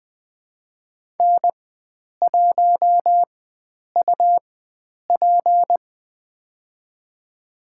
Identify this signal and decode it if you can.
Morse code, heard as a keyed tone, about 21 wpm, spaced as for 9 wpm, 700 Hz. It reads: N1UP
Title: USA Call Signs - 4 Characters - 4x Character Spacing - 20wpm